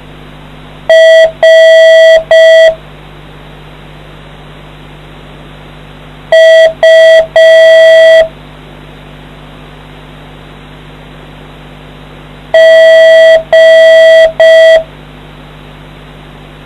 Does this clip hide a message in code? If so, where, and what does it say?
Yes: Morse code from 0.9 to 14.8 s, heard as a keyed tone, about 5 wpm, 650 Hz. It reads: R U G